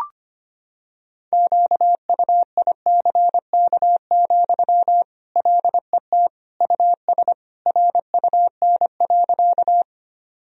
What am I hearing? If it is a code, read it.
Morse code, heard as a keyed tone, about 25 wpm, 700 Hz. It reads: QUICK, LET US RUN.
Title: Sentences from Top 1000 Words - Review Part 1 - 25wpm